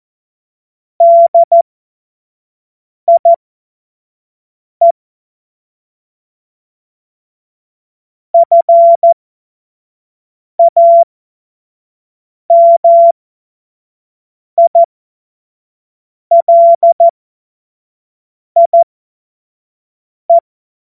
Morse code, heard as a keyed tone, about 14 wpm, spaced as for 5 wpm, 680 Hz. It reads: DIE FAMILIE
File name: Morsekurs / Witze / Witz 21. März 2023 25 F70